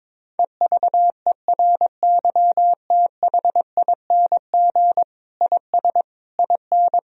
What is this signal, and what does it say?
Morse code, heard as a keyed tone, about 22 wpm, 700 Hz. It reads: EVERYTHING IS IN
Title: Sentences from Top 1000 Words - Review Part 2 - 22wpm